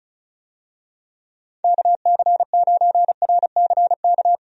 Morse code, heard as a keyed tone, about 34 wpm, 700 Hz. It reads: KC9RCK
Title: USA Call Signs - 6 Characters - 35wpm